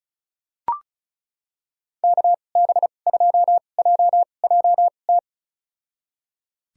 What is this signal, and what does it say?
Morse code, heard as a keyed tone, about 36 wpm, spaced as for 25 wpm, 700 Hz. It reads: KB2JJT
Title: USA Call Signs - 6 Characters - 2x Character Spacing - 35wpm